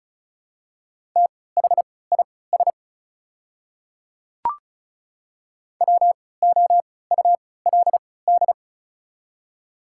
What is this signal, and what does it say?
Morse code, heard as a keyed tone, about 36 wpm, spaced as for 20 wpm, 700 Hz. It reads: THIS WOULD
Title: 3x Character Spacing - Top 100 Words - 35wpm